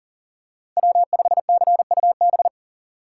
Morse code, heard as a keyed tone, about 39 wpm, 700 Hz. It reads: W5CUB
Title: USA Call Signs - 5 Characters - 4x Character Spacing - 40wpm